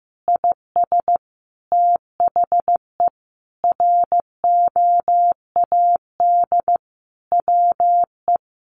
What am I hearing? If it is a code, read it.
Morse code, heard as a keyed tone, about 15 wpm, 700 Hz. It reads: IS THE ROAD WE
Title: Sentences from Top 1100 Words 15wpm